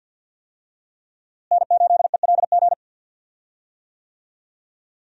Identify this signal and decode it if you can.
Morse code, heard as a keyed tone, about 52 wpm, 700 Hz. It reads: N8ELG
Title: USA Call Signs - 5 Characters - Repeated 3x - 50wpm